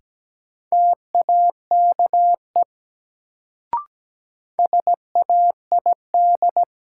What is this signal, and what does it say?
Morse code, heard as a keyed tone, about 17 wpm, 700 Hz. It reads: TAKE SAID
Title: Sets of 2 Words - New Word Drill - 17wpm